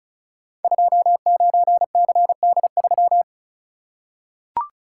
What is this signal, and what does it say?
Morse code, heard as a keyed tone, about 36 wpm, 690 Hz. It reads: 29CD3
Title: Code Groups - 5 Letter:Numbers - 35wpm